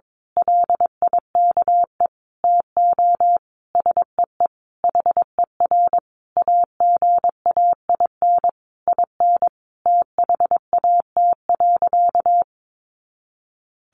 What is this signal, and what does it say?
Morse code, heard as a keyed tone, about 22 wpm, 700 Hz. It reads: LIKE TO SEE HER AGAIN IN THAT.
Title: Sentences from Top 200 Words 22wpm